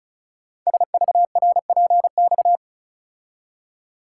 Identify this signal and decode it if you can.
Morse code, heard as a keyed tone, about 36 wpm, 700 Hz. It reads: SVRPX
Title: Code Groups - 5 Letters - 35wpm